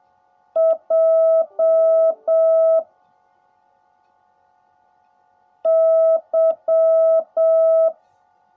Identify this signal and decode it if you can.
Morse code, heard as a keyed tone, about 7 wpm, 660 Hz. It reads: J Y
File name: jh8cbh / 2021cw / cw16a